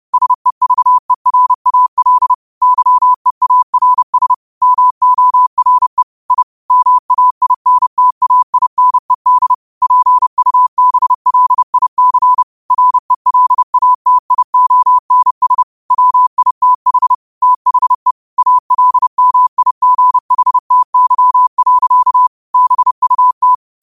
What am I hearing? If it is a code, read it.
Morse code, heard as a keyed tone, about 30 wpm, 1000 Hz. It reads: SEVERAL YEARS MORE I MAINTAINED PUBLIC RELATIONS WITH THE ALMIGHTY. BUT